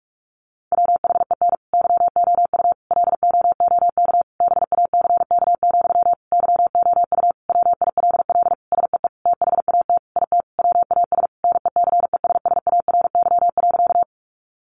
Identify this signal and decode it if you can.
Morse code, heard as a keyed tone, about 44 wpm, 700 Hz. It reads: WHEN YOU LOOK BACK, YOU WILL SEE THAT IT WAS NECESSARY.
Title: Sentences from Top 800 Words 45wpm